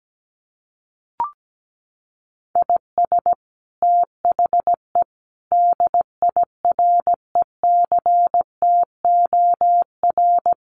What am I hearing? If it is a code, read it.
Morse code, heard as a keyed tone, about 17 wpm, 700 Hz. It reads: IS THE DIRECTOR
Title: Sets of 5 Words Encore2 17wpm